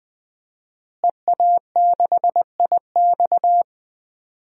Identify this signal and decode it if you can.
Morse code, heard as a keyed tone, about 20 wpm, 700 Hz. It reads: EA6IX
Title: DXCC-100 Call Signs - 5 Characters - Repeated 3x - 20wpm